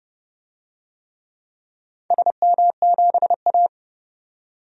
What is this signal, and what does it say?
Morse code, heard as a keyed tone, about 30 wpm, 700 Hz. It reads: SM7A
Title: DXCC-100 Call Signs - 4 Characters - 30wpm